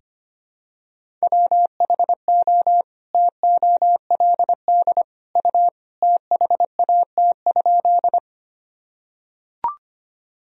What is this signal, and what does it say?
Morse code, heard as a keyed tone, about 25 wpm, 700 Hz. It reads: WHO TOLD U THAT?
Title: Sentences from Top 400 Words and Abbreviations 25wpm